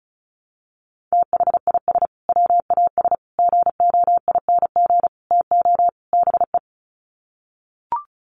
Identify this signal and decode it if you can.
Morse code, heard as a keyed tone, about 34 wpm, 700 Hz. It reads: THIS WAS GOING TO BE
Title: Sets of 5 Words Encore2 35wpm